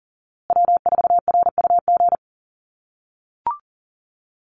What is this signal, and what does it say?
Morse code, heard as a keyed tone, about 40 wpm, 700 Hz. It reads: W4RUG
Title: USA Call Signs Repeated 2x 40wpm